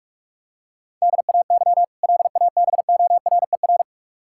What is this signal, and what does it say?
Morse code, heard as a keyed tone, about 41 wpm, 700 Hz. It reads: DAY LABORER
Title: K1USN SST QSO Practice 45wpm